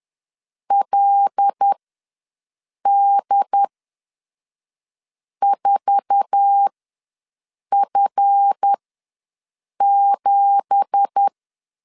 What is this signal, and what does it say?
Morse code, heard as a keyed tone, about 10 wpm, spaced as for 5 wpm, 800 Hz. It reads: LD4F7
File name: LEZIONI / 9 2867 ws 10wpm 4215